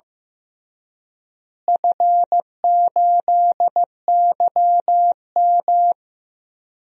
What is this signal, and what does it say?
Morse code, heard as a keyed tone, about 15 wpm, 700 Hz. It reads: F8YM
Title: DXCC-100 Call Signs - 4 Characters - Repeated 3x - 15wpm